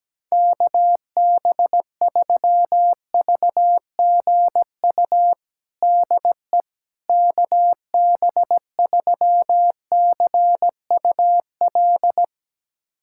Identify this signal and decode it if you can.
Morse code, heard as a keyed tone, about 17 wpm, 700 Hz. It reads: KB3VGU DE KB3CUL
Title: USA Callsign DE Callsign 17wpm